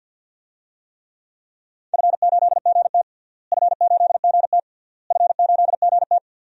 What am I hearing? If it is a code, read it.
Morse code, heard as a keyed tone, about 52 wpm, 700 Hz. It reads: F8GT F8GT F8GT
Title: DXCC-100 Call Signs - 4 Characters - Repeated 3x - 50wpm